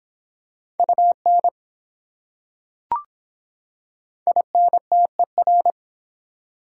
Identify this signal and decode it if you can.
Morse code, heard as a keyed tone, about 23 wpm, 700 Hz. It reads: UN INTER